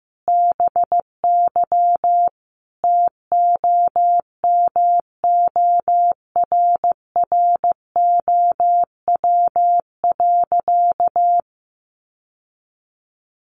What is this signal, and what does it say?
Morse code, heard as a keyed tone, about 15 wpm, 700 Hz. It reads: BY TOMORROW.
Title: Sentences from Top 1100 Words 15wpm